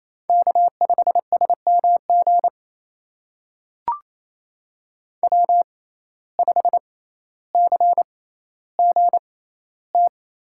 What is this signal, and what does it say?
Morse code, heard as a keyed tone, about 28 wpm, spaced as for 14 wpm, 700 Hz. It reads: K5SMG W5CGT
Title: USA Call Signs - 5 Characters - 6x Character Spacing - 28wpm